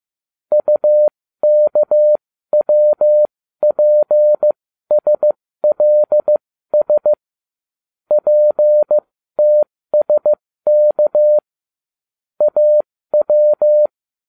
Morse code, heard as a keyed tone, about 15 wpm, spaced as for 13 wpm, 600 Hz. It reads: UKWPSLS PTSK AW